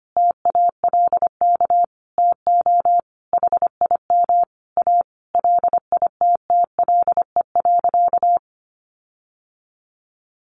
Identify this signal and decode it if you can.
Morse code, heard as a keyed tone, about 25 wpm, 700 Hz. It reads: TALK TO HIM A LITTLE.
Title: Sentences from Top 1500 Words 25wpm